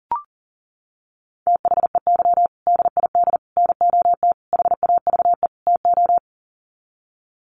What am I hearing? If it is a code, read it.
Morse code, heard as a keyed tone, about 40 wpm, 700 Hz. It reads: THEY DID NOT HAVE TO
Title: Sets of 5 Words Encore2 40wpm